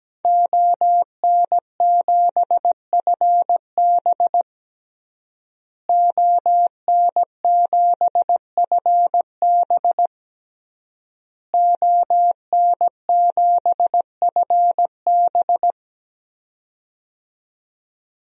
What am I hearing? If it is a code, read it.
Morse code, heard as a keyed tone, about 17 wpm, 700 Hz. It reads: ON7FB ON7FB ON7FB
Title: DXCC-100 Call Signs - 5 Characters - Repeated 3x - 17wpm